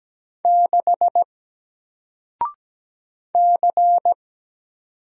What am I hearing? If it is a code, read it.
Morse code, heard as a keyed tone, about 17 wpm, 700 Hz. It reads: 6 C
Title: Single Character (Letter:Number:Punctuation:Prosign) 17wpm